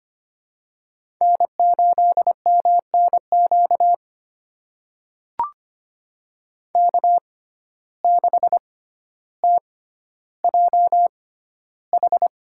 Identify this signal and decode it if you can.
Morse code, heard as a keyed tone, about 25 wpm, spaced as for 13 wpm, 700 Hz. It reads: N8MNQ K6TJH
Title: USA Call Signs - 6x Character Spacing - 25wpm